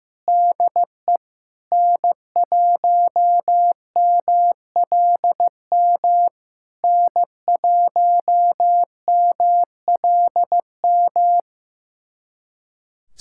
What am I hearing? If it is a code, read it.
Morse code, heard as a keyed tone, about 15 wpm, 700 Hz. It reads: DE N1MLM N1MLM
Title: SOTA QSO Practice 15wpm